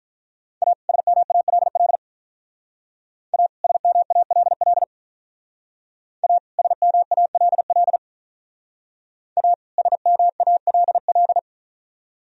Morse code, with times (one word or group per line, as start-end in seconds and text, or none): 0.6-0.7 A
0.9-2.0 SMALL
3.3-3.5 A
3.6-4.8 SMALL
6.2-6.4 A
6.6-8.0 SMALL
9.4-9.5 A
9.8-11.4 SMALL